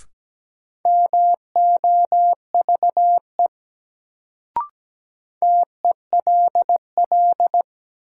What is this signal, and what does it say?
Morse code, heard as a keyed tone, about 17 wpm, 700 Hz. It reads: MOVE TELL
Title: Sentences from Top 200 Words - New Word Drill - 17wpm